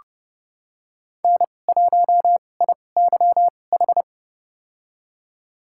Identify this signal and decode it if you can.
Morse code, heard as a keyed tone, about 32 wpm, spaced as for 22 wpm, 700 Hz. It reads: N1IYH
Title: USA Call Signs - 2x Character Spacing - 30wpm